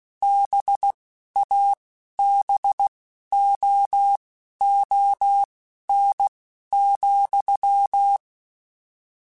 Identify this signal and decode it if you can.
Morse code, heard as a keyed tone, about 16 wpm, spaced as for 11 wpm, 800 Hz. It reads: BABOON,
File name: Misc / Misc-mp3s / BIG @10